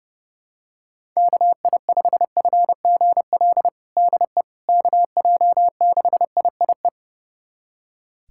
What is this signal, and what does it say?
Morse code, heard as a keyed tone, about 32 wpm, 700 Hz. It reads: KI5FGL DE KJ6IIE
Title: USA Callsign DE Callsign - 6 Characters - 30wpm